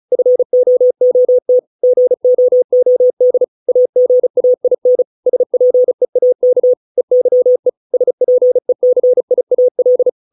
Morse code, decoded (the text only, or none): FOOT GOOD AGAIN SPEAK EYE SPECIAL